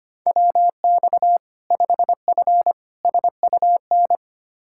Morse code, heard as a keyed tone, about 25 wpm, 700 Hz. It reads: WX 5F SUN
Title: Practice QSO - Exchange 2 - Info - 25wpm